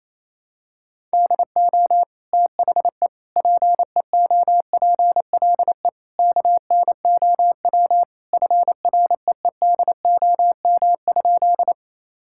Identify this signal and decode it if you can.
Morse code, heard as a keyed tone, about 28 wpm, 700 Hz. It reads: DO THE PEOPLE KNOW FREEDOM?
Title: Sentences from Top 1000 Words - Review Part 5 - 28wpm